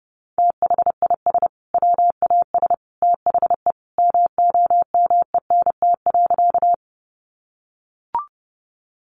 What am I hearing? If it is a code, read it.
Morse code, heard as a keyed tone, about 30 wpm, 700 Hz. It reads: THIS WAS THE MOMENT.
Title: Sentences from Top 1000 Words 30wpm